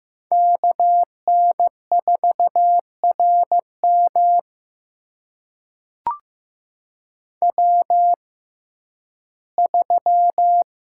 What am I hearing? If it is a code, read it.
Morse code, heard as a keyed tone, about 15 wpm, spaced as for 11 wpm, 700 Hz. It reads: KN4RM W3
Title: USA Call Signs - 5 Characters - 6x Character Spacing - 15wpm